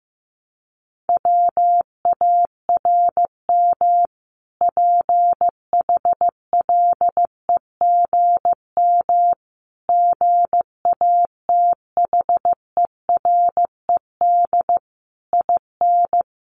Morse code, heard as a keyed tone, about 15 wpm, 700 Hz. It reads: WARM PHLEGM GATHERED IN
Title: A Sound Of Thunder 15wpm